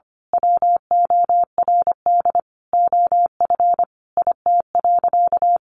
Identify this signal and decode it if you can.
Morse code, heard as a keyed tone, about 25 wpm, 700 Hz. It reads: WORD OF IT.
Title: Sentences from Top 1200 Words 25wpm